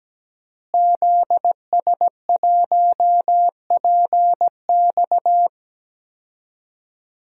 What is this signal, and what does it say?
Morse code, heard as a keyed tone, about 17 wpm, 700 Hz. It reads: ZS1PX
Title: DXCC-100 Call Signs - 5 Characters - Repeated 2x - 17wpm